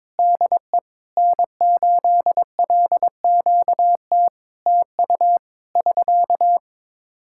Morse code, heard as a keyed tone, about 22 wpm, 700 Hz. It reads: DE N8LQT TU <SK>
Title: Practice QSO - Exchange 4 - Ending - 22wpm